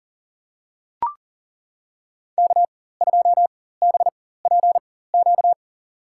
Morse code, heard as a keyed tone, about 40 wpm, spaced as for 19 wpm, 700 Hz. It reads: K2BPQ
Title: USA Call Signs - 5 Characters - 4x Character Spacing - 40wpm